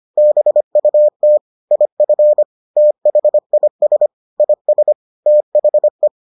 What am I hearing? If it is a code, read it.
Morse code, heard as a keyed tone, about 25 wpm, 600 Hz. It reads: BUT IF THIS IS THE